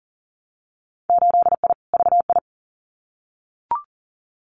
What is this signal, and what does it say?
Morse code, heard as a keyed tone, about 38 wpm, 700 Hz. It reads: 8I VI
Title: Winter Field Day Exchanges Encore 40wpm